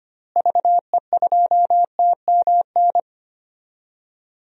Morse code, heard as a keyed tone, about 25 wpm, 700 Hz. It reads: VE2TMN
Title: DXCC-100 Call Signs - 6 Characters - Repeated 2x - 25wpm